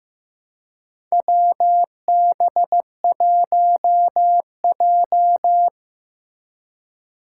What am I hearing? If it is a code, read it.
Morse code, heard as a keyed tone, about 15 wpm, 700 Hz. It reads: WB1J
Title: USA Call Signs - 4 Characters - 4x Character Spacing - 15wpm